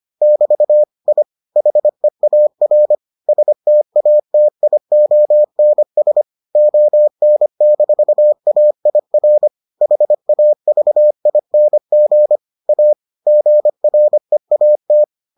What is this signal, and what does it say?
Morse code, heard as a keyed tone, about 25 wpm, 600 Hz. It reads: <BT> I HEAR STATIONS ON-AIR HAVING A GREAT